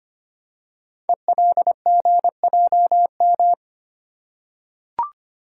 Morse code, heard as a keyed tone, about 25 wpm, 700 Hz. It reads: ELGJM
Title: Code Groups - 5 Letters - 25wpm